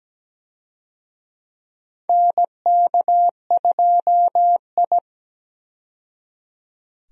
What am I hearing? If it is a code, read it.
Morse code, heard as a keyed tone, about 17 wpm, 700 Hz. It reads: NK2I
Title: USA Call Signs - 4 Characters - 17wpm